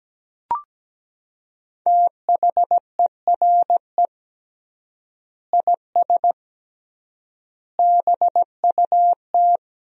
Morse code, heard as a keyed tone, about 17 wpm, 700 Hz. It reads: THERE IS BUT
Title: 3x Word-Spacing - Sentences from Top 100 Words - 17wpm